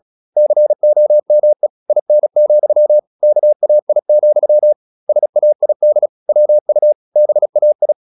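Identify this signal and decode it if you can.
Morse code, heard as a keyed tone, about 36 wpm, 600 Hz. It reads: COME IN, KAI, SAID WU BAI
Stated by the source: Morse practice at 36 wpm, text of Ken Follett's novel Never, from CW med Gnister.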